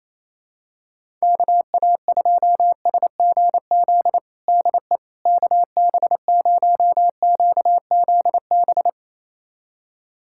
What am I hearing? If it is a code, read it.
Morse code, heard as a keyed tone, about 28 wpm, 700 Hz. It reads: KA2SGZ DE KB0QZB
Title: USA Callsign DE Callsign 28wpm